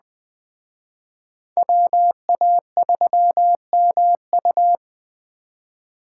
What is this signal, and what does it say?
Morse code, heard as a keyed tone, about 20 wpm, 700 Hz. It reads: WA3MU
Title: Speed Racing - USA 2x2 Call Signs 17wpm